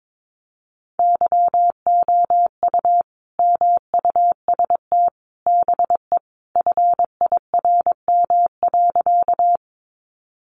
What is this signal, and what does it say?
Morse code, heard as a keyed tone, about 22 wpm, 700 Hz. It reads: YOU MUST BE FIRM.
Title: Sentences from Top 1400 Words 22wpm